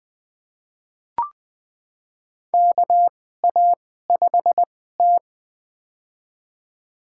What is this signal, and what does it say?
Morse code, heard as a keyed tone, about 20 wpm, spaced as for 14 wpm, 700 Hz. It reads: KA5T
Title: USA Call Signs - 4 Characters - 2x Character Spacing - 20wpm